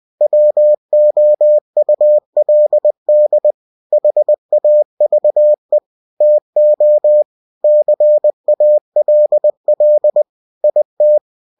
Morse code, heard as a keyed tone, about 20 wpm, 600 Hz. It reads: WOULD HAVE TO CALL IT
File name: cw-books / winnie-the-pooh / winnie-20wpm-page-0074